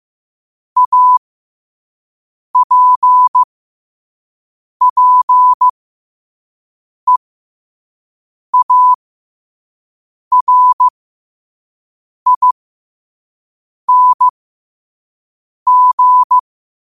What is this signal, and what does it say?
Morse code, heard as a keyed tone, about 15 wpm, spaced as for 5 wpm, 1000 Hz. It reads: APPEARING
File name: qotdfiles / 5wpm